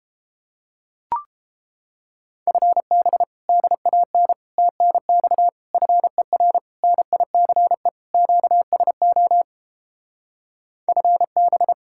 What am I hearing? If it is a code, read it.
Morse code, heard as a keyed tone, about 31 wpm, 700 Hz. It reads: FB DAN TNX FER NICE QSO FB